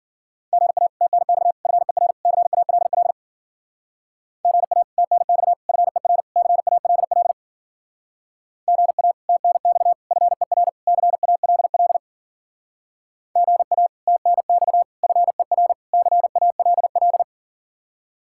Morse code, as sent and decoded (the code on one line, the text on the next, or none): --. .- / - -. -..- / ..-. .-. / -.-. .- .-.. .-.. / --. .- / - -. -..- / ..-. .-. / -.-. .- .-.. .-.. / --. .- / - -. -..- / ..-. .-. / -.-. .- .-.. .-.. / --. .- / - -. -..- / ..-. .-. / -.-. .- .-.. .-..
GA TNX FER CALL GA TNX FER CALL GA TNX FER CALL GA TNX FER CALL